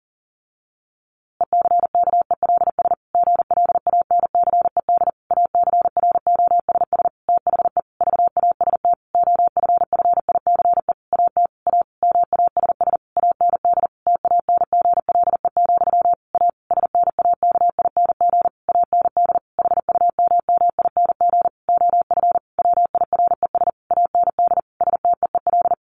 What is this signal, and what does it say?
Morse code, heard as a keyed tone, about 40 wpm, 700 Hz. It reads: ECKELS GLANCED ACROSS THE VAST OFFICE AT A MASS AND TANGLE, A SNAKING AND HUMMING OF WIRES AND STEEL
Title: A Sound Of Thunder 40wpm